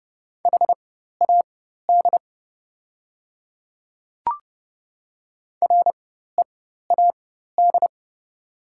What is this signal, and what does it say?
Morse code, heard as a keyed tone, about 30 wpm, spaced as for 14 wpm, 700 Hz. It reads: HAD READ
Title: 4x Character Spacing - Top 200 Words - 30wpm